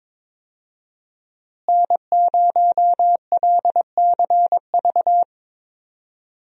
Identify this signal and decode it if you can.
Morse code, heard as a keyed tone, about 22 wpm, 700 Hz. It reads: N0LCV
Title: USA Call Signs - 5 Characters - Repeated 3x - 22wpm